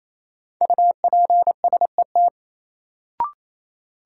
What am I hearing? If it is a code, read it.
Morse code, heard as a keyed tone, about 28 wpm, 700 Hz. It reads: UPSET
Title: Sentences from Top 800 Words - New Word Drill - 28wpm